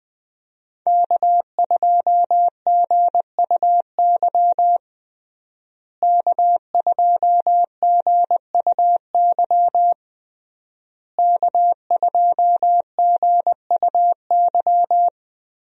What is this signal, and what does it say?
Morse code, heard as a keyed tone, about 20 wpm, 700 Hz. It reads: K2GUY K2GUY K2GUY
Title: USA Call Signs - 5 Characters - Repeated 3x - 20wpm